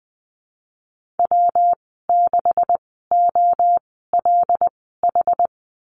Morse code, heard as a keyed tone, about 20 wpm, spaced as for 15 wpm, 700 Hz. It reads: W6OLH
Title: 147 Lesson 24g - Letter y - Call Signs 20wpm